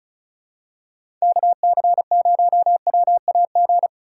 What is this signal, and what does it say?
Morse code, heard as a keyed tone, about 34 wpm, 700 Hz. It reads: KC0WAG
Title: USA Call Signs - 6 Characters - 3x Character Spacing - 35wpm